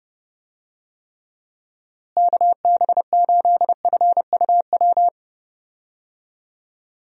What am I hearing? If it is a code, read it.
Morse code, heard as a keyed tone, about 31 wpm, 700 Hz. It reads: KB8FUW
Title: USA Call Signs - 6 Characters - Repeated 3x - 30wpm